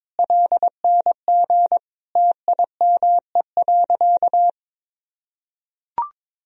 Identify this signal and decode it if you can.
Morse code, heard as a keyed tone, about 22 wpm, 700 Hz. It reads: LNG TIME.
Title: Sentences from Top 400 Words and Abbreviations 22wpm